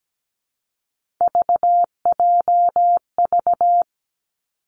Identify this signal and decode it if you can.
Morse code, heard as a keyed tone, about 17 wpm, 700 Hz. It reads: VJV
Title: Three Challenging Letter-Number Combinations 17wpm